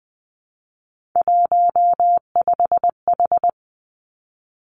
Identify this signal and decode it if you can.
Morse code, heard as a keyed tone, about 20 wpm, 700 Hz. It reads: 15H
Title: Three Challenging Letter-Number Combinations 20wpm